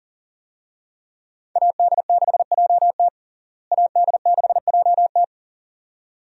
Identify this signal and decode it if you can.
Morse code, heard as a keyed tone, about 37 wpm, 700 Hz. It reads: AD6JT AD6JT
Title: USA Call Signs - 5 Characters - Repeated 2x - 40wpm